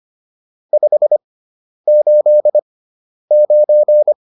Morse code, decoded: 589

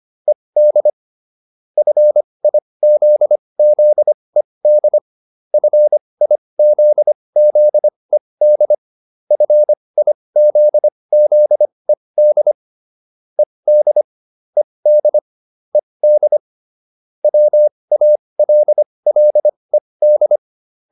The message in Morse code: ED FIZZED FIZZED FIZZED ED ED ED WALLED

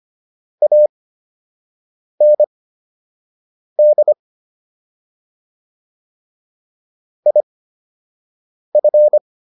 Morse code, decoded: AND IF

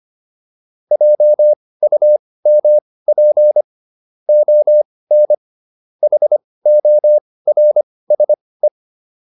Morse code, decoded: JUMP ON HORSE

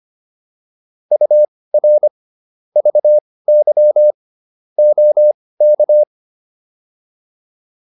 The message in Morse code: UR VY OK

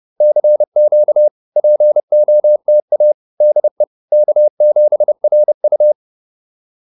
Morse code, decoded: CQ POTA DE K7RU

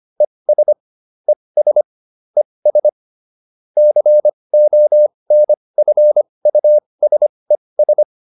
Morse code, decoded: ES ES ES CONFUSES

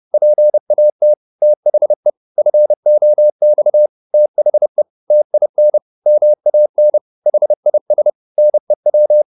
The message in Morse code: PAT THE FOX THE TIN MAN HIS NEW

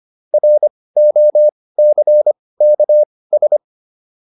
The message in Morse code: ROCKS

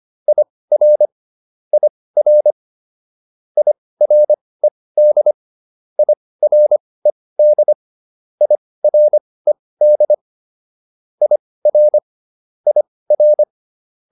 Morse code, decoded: IR IR IRED IRED IRED IR IR

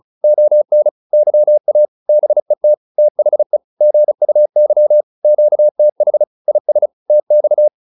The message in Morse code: ON YA BET THE GUY QTH IS TX

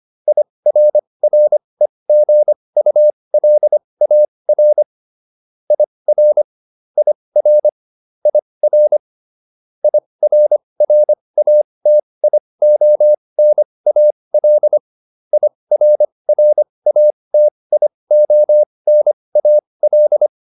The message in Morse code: IRREGULAR IR IR IR IRRATIONAL IRRATIONAL

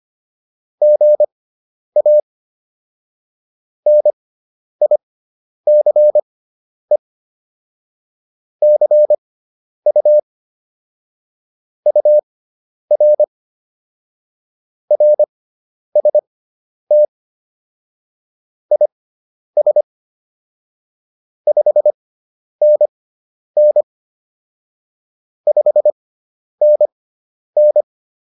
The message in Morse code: GA NICE CU UR RST IS 5NN 5NN

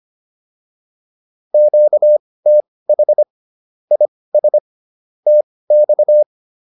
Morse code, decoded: QTH IS TX

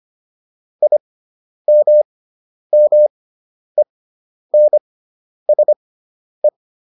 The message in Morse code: IMMENSE

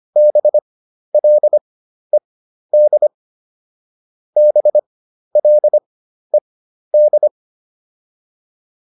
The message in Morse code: BLED BLED

